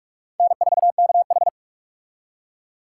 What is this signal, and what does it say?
Morse code, heard as a keyed tone, about 46 wpm, 700 Hz. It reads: N4KH
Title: K1USN SST QSO Practice - 45wpm